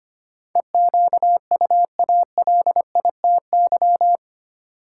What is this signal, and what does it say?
Morse code, heard as a keyed tone, about 25 wpm, 700 Hz. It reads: EQUALITY
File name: 8 Letter Words - First 3 Letters Given - 25wpm